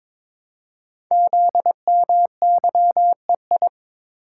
Morse code, heard as a keyed tone, about 22 wpm, 700 Hz. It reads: ZMYEI
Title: Code Groups - 5 Letters - 22wpm